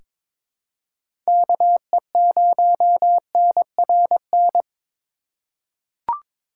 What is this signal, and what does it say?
Morse code, heard as a keyed tone, about 22 wpm, 700 Hz. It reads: KE0NRN
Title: USA Call Signs - 6 Characters - 3x Character Spacing - 22wpm